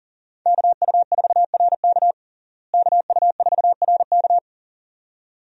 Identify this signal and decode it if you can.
Morse code, heard as a keyed tone, about 40 wpm, 700 Hz. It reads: KU4RK KU4RK
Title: USA Call Signs - 5 Characters - Repeated 2x - 40wpm